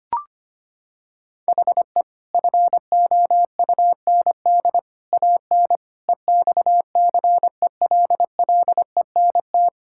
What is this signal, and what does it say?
Morse code, heard as a keyed tone, about 25 wpm, 700 Hz. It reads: HE FOUND AN EXCELLENT
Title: Sentences from Top 1000 Words - Review Part 6 - 25wpm